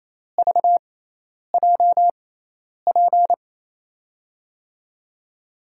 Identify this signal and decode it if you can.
Morse code, heard as a keyed tone, about 28 wpm, spaced as for 10 wpm, 700 Hz. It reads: VJP
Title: USA Call Signs - 6 Characters - 6x Character Spacing - 28wpm